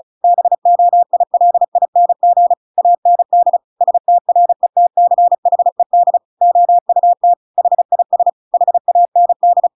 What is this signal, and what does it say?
Morse code, heard as a keyed tone, about 36 wpm, 700 Hz. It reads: BOILING AND STRETCHED OUT HIS HAND
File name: Pinocchio / Pinocchio ch5 35 35 ik1hge